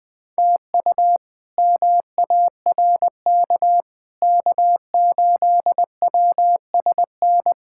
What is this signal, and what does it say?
Morse code, heard as a keyed tone, about 20 wpm, 700 Hz. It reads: TU MARK K8WSN
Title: K1USN SST QSO Practice - 20wpm